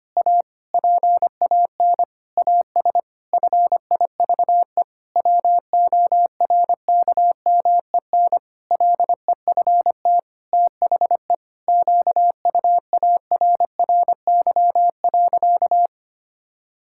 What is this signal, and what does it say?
Morse code, heard as a keyed tone, about 25 wpm, 700 Hz. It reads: A PAN AS FIVE WORKMEN LEFT THE QUARRY.
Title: Pangram Challenge v2 - 25wpm